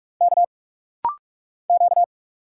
Morse code, heard as a keyed tone, about 46 wpm, 700 Hz. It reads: K Q
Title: Single Letter-Number Alt - Rapid-Fire 45wpm